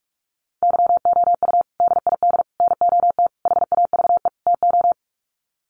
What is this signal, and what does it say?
Morse code, heard as a keyed tone, about 44 wpm, 700 Hz. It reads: YOU DID NOT HAVE TO
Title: Sets of 5 Words Encore 45wpm